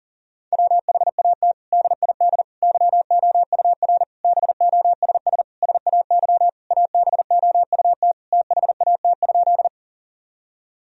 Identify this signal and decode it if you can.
Morse code, heard as a keyed tone, about 40 wpm, 700 Hz. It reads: WHAT DID YOUR BOSS SAY ABOUT THAT?
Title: Sentences from Top 1000 Words - Review Part 4 - 40wpm